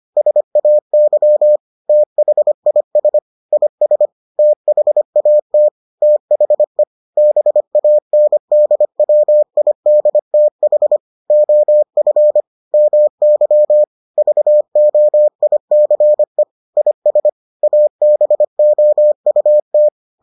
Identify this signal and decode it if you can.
Morse code, heard as a keyed tone, about 25 wpm, 600 Hz. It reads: SAY THIS IS THAT THE BANDWIDTH OF MY VOICE IS ABOUT